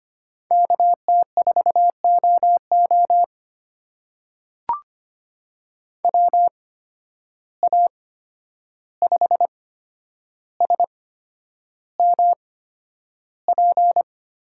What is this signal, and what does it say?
Morse code, heard as a keyed tone, about 25 wpm, spaced as for 10 wpm, 700 Hz. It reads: KT4OO WA5SMP
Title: USA Call Signs - 8x Character Spacing - 25wpm